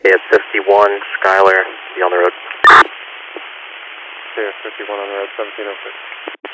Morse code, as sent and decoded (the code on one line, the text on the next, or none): none